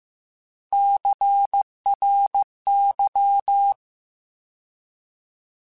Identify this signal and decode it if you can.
Morse code, heard as a keyed tone, about 15 wpm, 800 Hz. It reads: CRY
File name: Alphabetical / Alphabetical mp3s / Cs @13